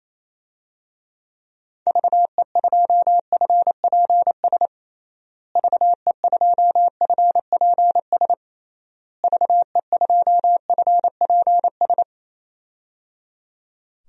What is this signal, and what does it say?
Morse code, heard as a keyed tone, about 28 wpm, 700 Hz. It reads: VE2FPS VE2FPS VE2FPS
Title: DXCC-100 Call Signs - 6 Characters - Repeated 3x - 28wpm